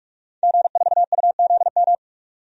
Morse code, heard as a keyed tone, about 46 wpm, 700 Hz. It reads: G4UZM
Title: DXCC-100 Call Signs - 5 Characters - 45wpm